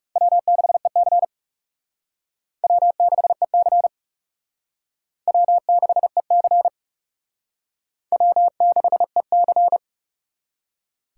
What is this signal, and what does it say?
Morse code, heard as a keyed tone, about 38 wpm, 700 Hz. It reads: W6EC W6EC W6EC W6EC